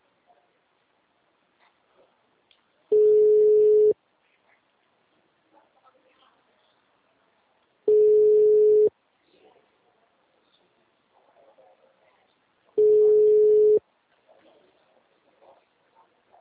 Ringback tone (Europe, 425 Hz).